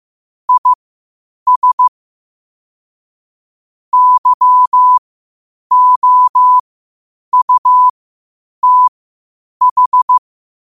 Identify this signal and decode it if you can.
Morse code, heard as a keyed tone, about 15 wpm, spaced as for 8 wpm, 1000 Hz. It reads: IS YOUTH
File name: qotdfiles / 8wpm